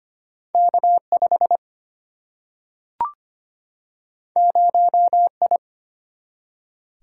Morse code, heard as a keyed tone, about 25 wpm, 700 Hz. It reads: K 5 0 I
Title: Single-Letter Single-Number 25wpm